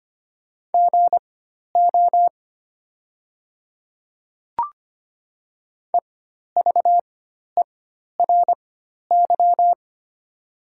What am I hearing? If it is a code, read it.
Morse code, heard as a keyed tone, about 25 wpm, spaced as for 12 wpm, 700 Hz. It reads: GO EVERY
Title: 4x Character Spacing - Top 200 Words - 25wpm